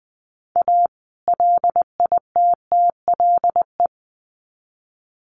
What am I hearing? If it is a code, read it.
Morse code, heard as a keyed tone, about 20 wpm, 700 Hz. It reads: A LITTLE
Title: Sets of 2 Words 20wpm